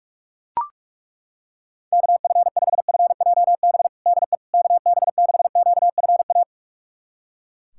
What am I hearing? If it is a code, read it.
Morse code, heard as a keyed tone, about 42 wpm, 690 Hz. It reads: KF5FJB DE KB6QFA
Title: USA Callsign DE Callsign - 6 Characters - 45wpm